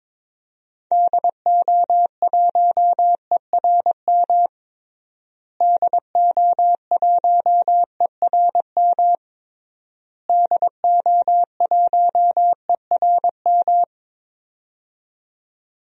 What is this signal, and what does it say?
Morse code, heard as a keyed tone, about 22 wpm, 700 Hz. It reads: DO1ERM DO1ERM DO1ERM
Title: DXCC-100 Call Signs - 6 Characters - Repeated 3x - 22wpm